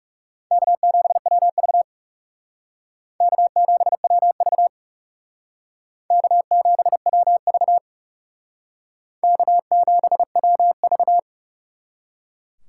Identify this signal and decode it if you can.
Morse code, heard as a keyed tone, about 37 wpm, 700 Hz. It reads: K7WV K7WV K7WV K7WV
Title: Speed Racing - USA Call Signs 30wpm